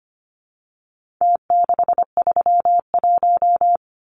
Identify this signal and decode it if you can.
Morse code, heard as a keyed tone, about 25 wpm, 700 Hz. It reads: T631
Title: Numbers - Contest Style - 4-Digit 25wpm